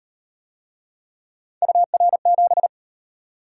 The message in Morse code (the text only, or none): UR7